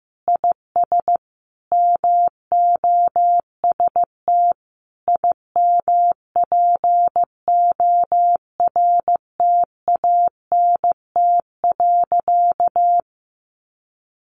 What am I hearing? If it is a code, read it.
Morse code, heard as a keyed tone, about 15 wpm, 700 Hz. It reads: IS MOST IMPORTANT.
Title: Sentences from Top 1300 Words 15wpm